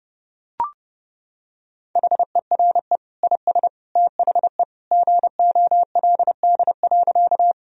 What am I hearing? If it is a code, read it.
Morse code, heard as a keyed tone, about 30 wpm, 700 Hz. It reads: HERE IS THE GOLD.
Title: Sentences from Top 1000 Words - Review Part 3 - 30wpm